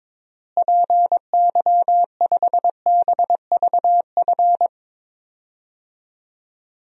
Morse code, heard as a keyed tone, about 22 wpm, 700 Hz. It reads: PY5BVF
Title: DXCC-100 Call Signs - 6 Characters - Repeated 3x - 22wpm